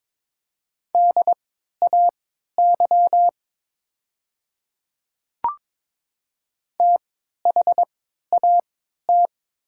Morse code, heard as a keyed tone, about 22 wpm, spaced as for 13 wpm, 700 Hz. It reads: DAY THAT
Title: 3x Character Spacing - Top 100 Words - 22wpm